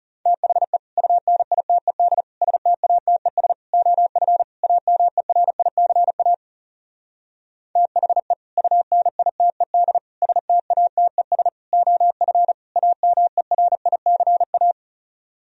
Morse code, heard as a keyed tone, about 40 wpm, 700 Hz. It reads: THE UNITED STATES OF AMERICA THE UNITED STATES OF AMERICA